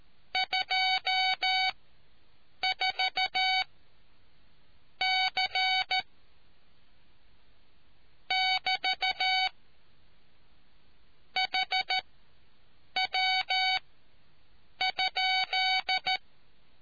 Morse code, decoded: 24 C <BT> HW?

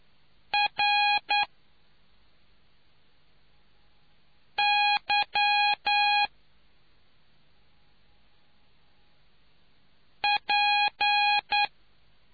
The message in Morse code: RY P

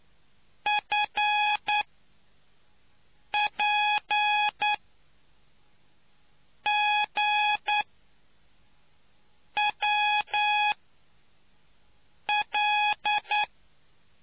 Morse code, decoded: FPGWL

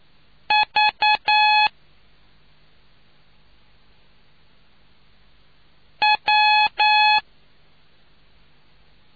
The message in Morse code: V W

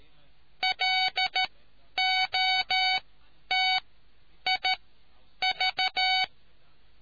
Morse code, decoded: LOTIV

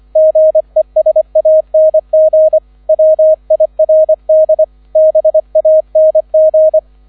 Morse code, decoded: GESANG WIRD BANG

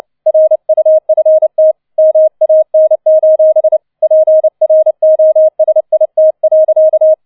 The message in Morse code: RUFT MAN: PROSIT.

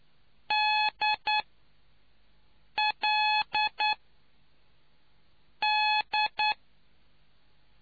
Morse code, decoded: D L D

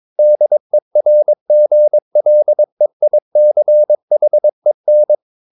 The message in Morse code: DERGLEICHEN

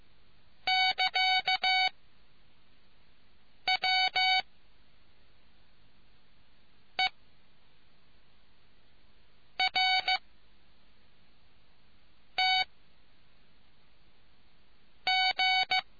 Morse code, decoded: <KA> W E R T G

